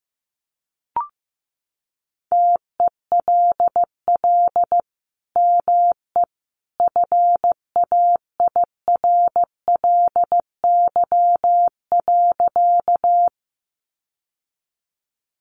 Morse code, heard as a keyed tone, about 15 wpm, 700 Hz. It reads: TELL ME FAIRLY.
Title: Sentences from Top 1100 Words 15wpm